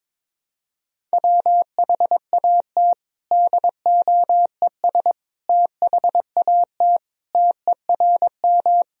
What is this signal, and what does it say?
Morse code, heard as a keyed tone, about 22 wpm, 700 Hz. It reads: WHAT DOES THAT TERM
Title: Sentences from Top 1000 Words - Review Part 4 - 22wpm